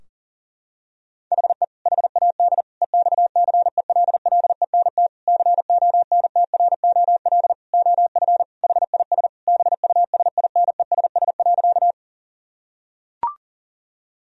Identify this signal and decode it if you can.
Morse code, heard as a keyed tone, about 40 wpm, 700 Hz. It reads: HE HAD EXCELLENT CONTROL OF HIS BUSINESS.
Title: Sentences from Top 1000 Words - Review Part 3 - 40wpm